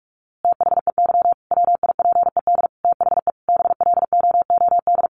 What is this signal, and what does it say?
Morse code, heard as a keyed tone, about 46 wpm, 700 Hz. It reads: THEY WIPED THE BLOOD
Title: A Sound Of Thunder 45wpm